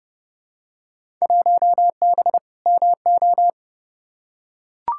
Morse code, heard as a keyed tone, about 30 wpm, 700 Hz. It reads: 1B MO
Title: Field Day Exchanges - 30wpm